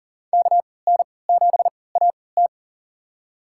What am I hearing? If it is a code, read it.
Morse code, heard as a keyed tone, about 40 wpm, spaced as for 23 wpm, 700 Hz. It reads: KN7AT
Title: USA Call Signs - 5 Characters - 3x Character Spacing - 40wpm